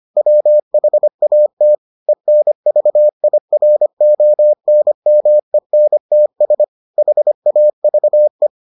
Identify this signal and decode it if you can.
Morse code, heard as a keyed tone, about 25 wpm, 600 Hz. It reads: WHAT ENVIRONMENTS HAVE